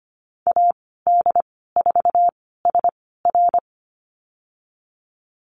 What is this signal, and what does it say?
Morse code, heard as a keyed tone, about 25 wpm, spaced as for 16 wpm, 700 Hz. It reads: AD4SR
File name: Lesson 15g - Number 5 - Call Signs 25wpm